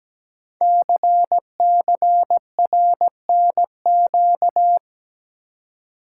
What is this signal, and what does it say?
Morse code, heard as a keyed tone, about 17 wpm, 700 Hz. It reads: CCRNQ
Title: Code Groups - 5 Letter:Numbers - 17wpm